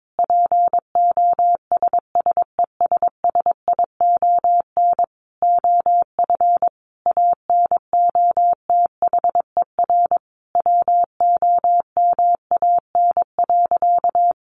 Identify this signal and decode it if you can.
Morse code, heard as a keyed tone, about 22 wpm, 700 Hz. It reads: POSSESSION OF ANOTHER WOMAN.
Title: Sentences from Top 1400 Words 22wpm